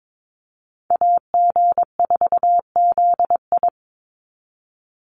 Morse code, heard as a keyed tone, about 22 wpm, 700 Hz. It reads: AG4ZI
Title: USA Call Signs 22wpm